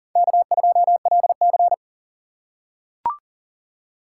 Morse code, heard as a keyed tone, about 40 wpm, 700 Hz. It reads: K2LC
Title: USA Call Signs - 4 Characters - 3x Character Spacing - 40wpm